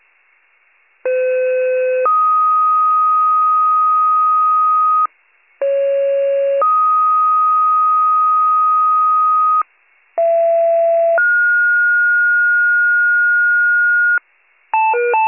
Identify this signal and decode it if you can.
Two-tone page, heard as a one-second tone, then a three-second tone, about 510 then 1250 Hz, about 550 then 1250 Hz, about 670 then 1430 Hz. It is heard from 1.0 to 14.2 s.